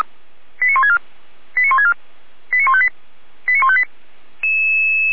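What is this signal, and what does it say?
Five-tone selective call (ZVEI), heard as quick runs of five tones, tones of about 70 ms. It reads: 89164 89164 89158 89158
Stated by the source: Five-tone ZVEI selective call received by SDR, recorded by feuerwehr-orlamuende fire brigade.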